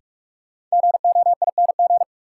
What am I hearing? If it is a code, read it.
Morse code, heard as a keyed tone, about 47 wpm, 700 Hz. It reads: GOING
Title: Sentences from Top 500 Words - New Word Drill - 45wpm